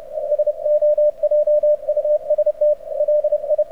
Morse code, heard as a keyed tone, about 31 wpm, 590 Hz. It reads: SO JUST LI
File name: archived / CRSnet